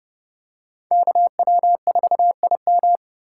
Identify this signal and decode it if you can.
Morse code, heard as a keyed tone, about 30 wpm, 700 Hz. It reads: KW4IM
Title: USA Call Signs - 5 Characters - 3x Character Spacing - 30wpm